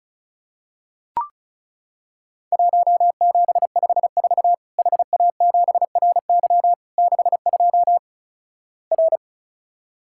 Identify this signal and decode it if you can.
Morse code, heard as a keyed tone, about 36 wpm, 700 Hz. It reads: 1754 HA7RY 62 R
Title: Worked All Europe DX Contest - QTC Exchange - 35wpm